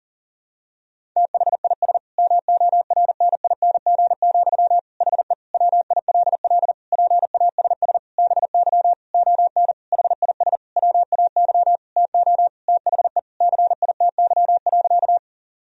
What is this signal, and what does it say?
Morse code, heard as a keyed tone, about 37 wpm, 700 Hz. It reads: THIS MORNING, HE WILL PASS BY ON HIS WAY TO THE CITY.
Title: Sentences from Top 1000 Words - Review Part 1 - 40wpm